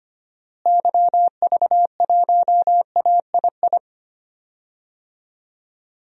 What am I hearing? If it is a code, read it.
Morse code, heard as a keyed tone, about 25 wpm, 700 Hz. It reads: YV1AII
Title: DXCC-100 Call Signs - 6 Characters - Repeated 2x - 25wpm